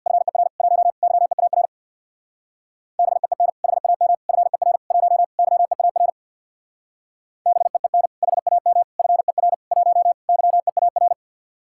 Morse code, heard as a keyed tone, about 62 wpm, 700 Hz. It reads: FER 1 YEAR BEEN HAM FER 1 YEAR BEEN HAM FER 1 YEAR
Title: Speed-Racing Common QSO Phrases - 50wpm